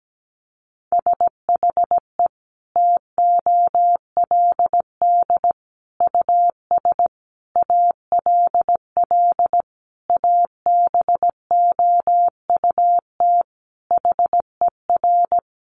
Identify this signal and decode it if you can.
Morse code, heard as a keyed tone, about 17 wpm, 700 Hz. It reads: SHE TOLD US ALL ABOUT HER
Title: Sentences from Top 1200 Words 17wpm